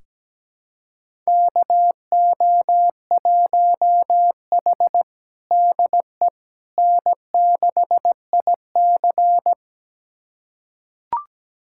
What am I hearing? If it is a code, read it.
Morse code, heard as a keyed tone, about 17 wpm, 700 Hz. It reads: KO1H DE N6IC
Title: USA Callsign DE Callsign - 4 Characters - 17wpm